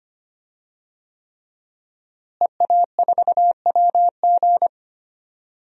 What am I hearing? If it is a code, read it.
Morse code, heard as a keyed tone, about 25 wpm, 700 Hz. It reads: EA4WG
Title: DXCC-100 Call Signs - 5 Characters - Repeated 3x - 25wpm